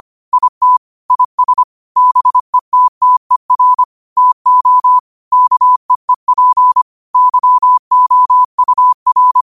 Morse code, heard as a keyed tone, about 25 wpm, 1000 Hz. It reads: IT IS BETTER TO KEEP YOUR
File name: qotdfiles / 25wpm